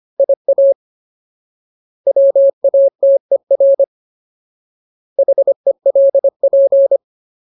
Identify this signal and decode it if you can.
Morse code, heard as a keyed tone, about 25 wpm, 550 Hz. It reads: IA WATER HELP
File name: Ultimate Word List / 25WPM Ultimate